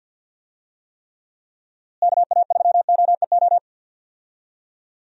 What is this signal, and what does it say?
Morse code, heard as a keyed tone, about 51 wpm, 700 Hz. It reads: KA3OEO
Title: USA Call Signs - 6 Characters - Repeated 2x - 50wpm